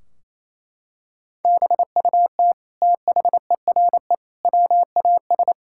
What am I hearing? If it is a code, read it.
Morse code, heard as a keyed tone, about 28 wpm, 700 Hz. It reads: BUT THERE WAS
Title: Sentences from Top 1600 Words 28wpm